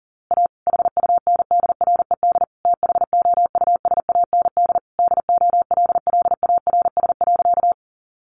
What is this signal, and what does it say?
Morse code, heard as a keyed tone, about 38 wpm, 700 Hz. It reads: A HUNDRED THOUSAND DOLLARS.
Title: A Sound Of Thunder 40wpm